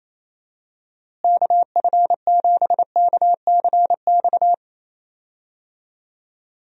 Morse code, heard as a keyed tone, about 28 wpm, 700 Hz. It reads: KF7KCX
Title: USA Call Signs - 6 Characters - Repeated 3x - 28wpm